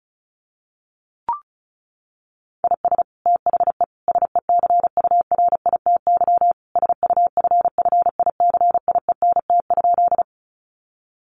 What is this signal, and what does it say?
Morse code, heard as a keyed tone, about 34 wpm, 700 Hz. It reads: IS THE SECURITY SUFFICIENT?
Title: Sentences from Top 1400 Words 35wpm